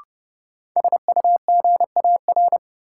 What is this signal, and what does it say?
Morse code, heard as a keyed tone, about 30 wpm, 700 Hz. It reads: SUGAR
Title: Sentences from Top 1200 Words - New Word Drill - 30wpm